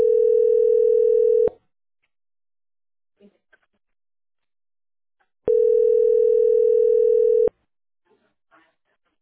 Ringback tone (North America), 440 + 480 Hz.